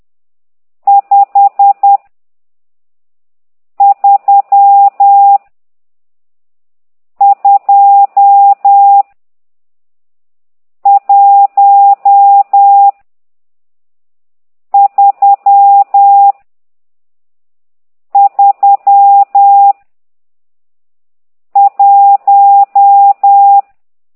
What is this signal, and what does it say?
Morse code, heard as a keyed tone, about 10 wpm, 800 Hz. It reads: 5 3 2 1 3 3 1
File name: ct2idw / exerc12